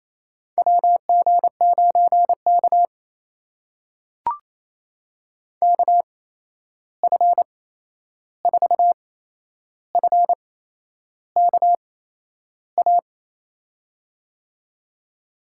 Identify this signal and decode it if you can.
Morse code, heard as a keyed tone, about 28 wpm, spaced as for 11 wpm, 700 Hz. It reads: WG9K KF4FKA
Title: USA Call Signs - 8x Character Spacing - 28wpm